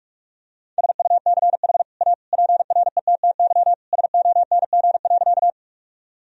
Morse code, heard as a keyed tone, about 46 wpm, 700 Hz. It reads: SUCH A PRETTY SONG.
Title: Sentences from Top 1000 Words - Review Part 2 - 45wpm